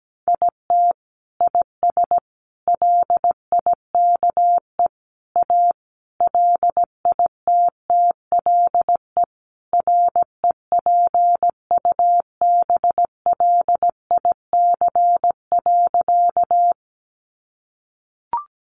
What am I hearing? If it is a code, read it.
Morse code, heard as a keyed tone, about 17 wpm, 700 Hz. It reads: IT IS LIKE A LITTLE REPUBLIC.
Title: Sentences from Top 1500 Words 17wpm